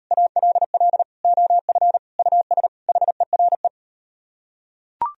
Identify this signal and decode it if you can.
Morse code, heard as a keyed tone, about 39 wpm, 700 Hz. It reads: ALL OF US HERE